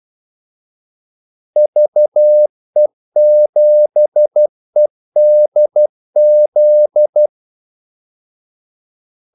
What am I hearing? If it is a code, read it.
Morse code, heard as a keyed tone, about 12 wpm, 600 Hz. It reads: VE7EDZ
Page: 2 ekstra ordmellemrum mellem kaldesignalerne